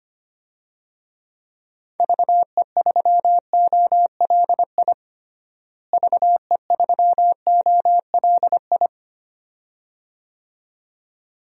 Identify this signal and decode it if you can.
Morse code, heard as a keyed tone, about 25 wpm, 700 Hz. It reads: VE3OLI VE3OLI
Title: DXCC-100 Call Signs - 6 Characters - Repeated 2x - 25wpm